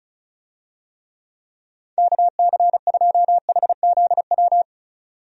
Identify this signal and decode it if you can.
Morse code, heard as a keyed tone, about 34 wpm, 700 Hz. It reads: KC2HZW
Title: USA Call Signs - 6 Characters - 35wpm